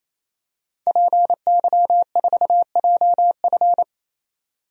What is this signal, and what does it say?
Morse code, heard as a keyed tone, about 28 wpm, 700 Hz. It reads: PY4JF
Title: DXCC-100 Call Signs - 5 Characters - 28wpm